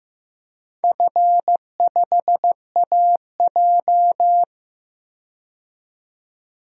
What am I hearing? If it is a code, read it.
Morse code, heard as a keyed tone, about 15 wpm, 700 Hz. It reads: F5AJ
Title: DXCC-100 Call Signs - 4 Characters - 15wpm